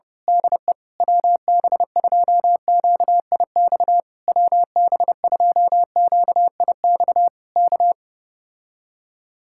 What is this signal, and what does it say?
Morse code, heard as a keyed tone, about 29 wpm, 700 Hz. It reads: DE WB2QIX WB2QIX K
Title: Speed-Racing Common QSO Phrases - 25wpm